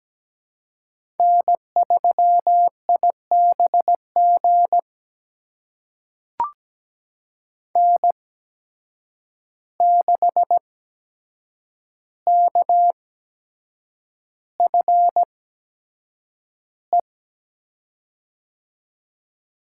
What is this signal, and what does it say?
Morse code, heard as a keyed tone, about 17 wpm, spaced as for 7 wpm, 700 Hz. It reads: N3IBG N6KFE
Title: USA Call Signs - 5 Characters - 8x Character Spacing - 17wpm